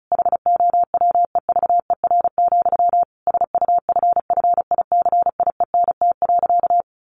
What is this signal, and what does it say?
Morse code, heard as a keyed tone, about 34 wpm, 700 Hz. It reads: HOWEVER, SUFFICIENT.
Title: Sentences from Top 1400 Words 35wpm